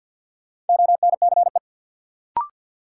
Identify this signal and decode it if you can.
Morse code, heard as a keyed tone, about 53 wpm, 700 Hz. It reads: ONCE
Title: Sets of 4 Words - New Word Drill - 50wpm